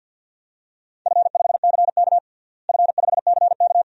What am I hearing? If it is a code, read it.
Morse code, heard as a keyed tone, about 52 wpm, 700 Hz. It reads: F5CK F5CK
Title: DXCC-100 Call Signs - 4 Characters - Repeated 2x - 50wpm